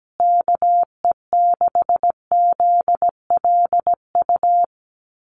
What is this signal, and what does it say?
Morse code, heard as a keyed tone, about 17 wpm, 700 Hz. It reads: KE6ZLU
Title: USA Call Signs Repeated 2x 17wpm